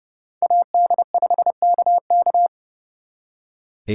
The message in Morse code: AD5KK

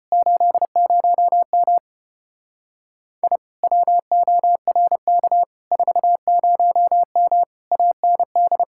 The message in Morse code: 80M I WORK 40M AND

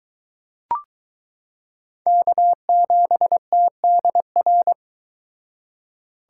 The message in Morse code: K7TDR

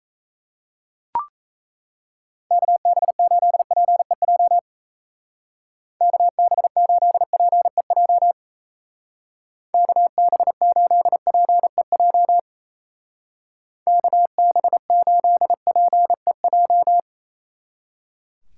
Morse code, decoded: KB8PEJ KB8PEJ KB8PEJ KB8PEJ

